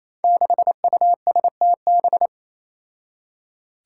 Morse code, heard as a keyed tone, about 28 wpm, 700 Hz. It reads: 6USTB
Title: Five Challenging Letter-Number Combinations 28wpm